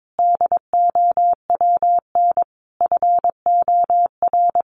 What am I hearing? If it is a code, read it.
Morse code, heard as a keyed tone, about 22 wpm, 700 Hz. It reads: DOWN FOR
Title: Sentences from Top 1500 Words 22wpm